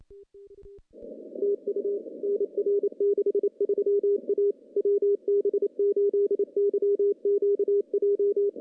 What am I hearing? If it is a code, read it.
Morse code, heard as a keyed tone, about 29 wpm, 410 Hz. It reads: TU NR 63 A WB8YQJ
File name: contests / CW-SS-k5tr- -0000